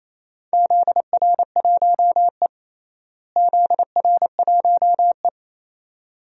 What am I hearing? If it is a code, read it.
Morse code, heard as a keyed tone, about 28 wpm, 700 Hz. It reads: ZR1E ZR1E
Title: DXCC-100 Call Signs - 4 Characters - Repeated 2x - 28wpm